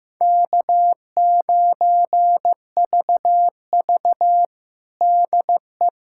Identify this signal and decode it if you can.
Morse code, heard as a keyed tone, about 15 wpm, 700 Hz. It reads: K9VV DE
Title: USA Callsign DE Callsign - 4 Characters - 15wpm